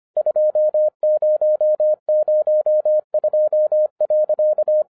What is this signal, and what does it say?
Morse code, heard as a keyed tone, about 25 wpm, 610 Hz. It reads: 2002.